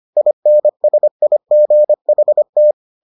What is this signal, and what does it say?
Morse code, heard as a keyed tone, about 25 wpm, 600 Hz. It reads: INSIGHT